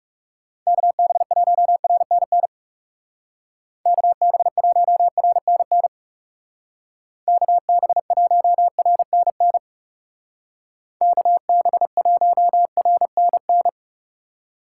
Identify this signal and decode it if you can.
Morse code, heard as a keyed tone, about 40 wpm, 700 Hz. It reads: KB1RNN KB1RNN KB1RNN KB1RNN